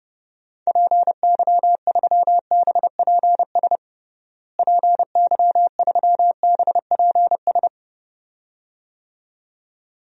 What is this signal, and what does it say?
Morse code, heard as a keyed tone, about 32 wpm, 700 Hz. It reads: PY3BPS PY3BPS
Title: DXCC-100 Call Signs - 6 Characters - Repeated 2x - 30wpm